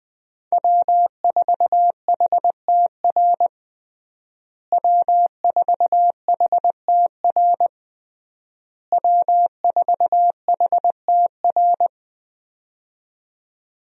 Morse code, heard as a keyed tone, about 21 wpm, 700 Hz. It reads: W4HTR W4HTR W4HTR
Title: USA Call Signs - 5 Characters - Repeated 3x - 20wpm